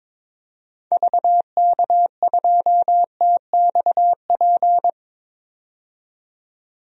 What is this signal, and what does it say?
Morse code, heard as a keyed tone, about 22 wpm, 700 Hz. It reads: VK2TXP
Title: DXCC-100 Call Signs - 6 Characters - 22wpm